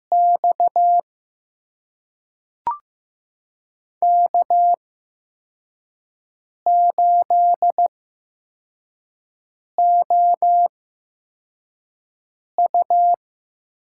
Morse code, heard as a keyed tone, about 15 wpm, spaced as for 4 wpm, 700 Hz. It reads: X K8OU
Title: USA Call Signs - 8x Character Spacing - 15wpm